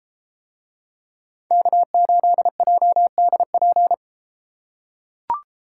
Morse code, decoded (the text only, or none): K8JDP